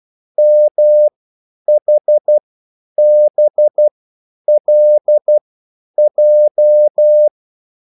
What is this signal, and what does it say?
Morse code, heard as a keyed tone, about 12 wpm, spaced as for 9 wpm, 600 Hz. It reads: MHBLJ